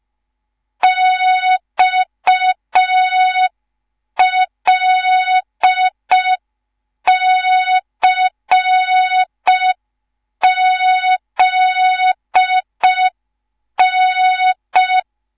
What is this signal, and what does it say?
Morse code, heard as a keyed tone, about 5 wpm, 3700 Hz. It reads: XLCZN